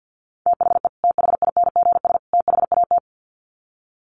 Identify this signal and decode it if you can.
Morse code, heard as a keyed tone, about 50 wpm, 700 Hz. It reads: THE THINGS THAT
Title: Sets of 3 Words 50wpm